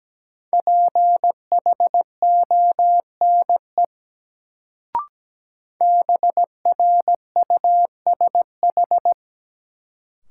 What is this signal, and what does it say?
Morse code, heard as a keyed tone, about 17 wpm, 700 Hz. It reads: PHONE BRUSH
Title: Sentences from Top 1100 Words - New Word Drill Alt - 17wpm